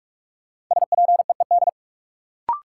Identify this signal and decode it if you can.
Morse code, heard as a keyed tone, about 46 wpm, 700 Hz. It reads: SPEED
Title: Sentences from Top 800 Words - New Word Drill - 45wpm